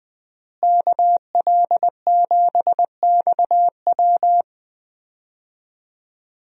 Morse code, heard as a keyed tone, about 20 wpm, 700 Hz. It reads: KL7XW
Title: DXCC-100 Call Signs - 5 Characters - Repeated 2x - 20wpm